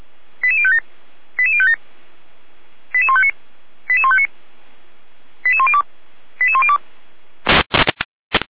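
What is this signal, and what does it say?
Five-tone selective call (ZVEI), heard as quick runs of five tones, tones of about 70 ms. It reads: 88057 88057 88159 88159 88102 88102
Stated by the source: ZVEI five-tone selective call 88437 received via SDR.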